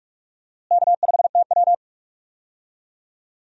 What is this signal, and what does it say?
Morse code, heard as a keyed tone, about 44 wpm, 700 Hz. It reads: K5TW
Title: USA Call Signs - 4 Characters - Repeated 3x - 45wpm